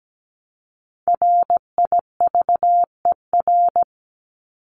Morse code, heard as a keyed tone, about 17 wpm, 700 Hz. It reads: RIVER
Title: Top 500 Words 17wpm